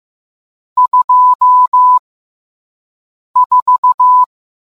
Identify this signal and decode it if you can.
Morse code, heard as a keyed tone, about 15 wpm, spaced as for 5 wpm, 1000 Hz. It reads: 24